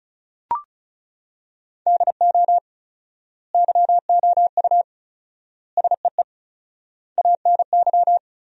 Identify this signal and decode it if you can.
Morse code, heard as a keyed tone, about 34 wpm, 700 Hz. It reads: DO YOU SEE ANY
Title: 4x Word-Spacing - Sentences from Top 500 Words - 35wpm